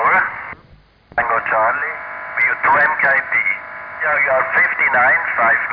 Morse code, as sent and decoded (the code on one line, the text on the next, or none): none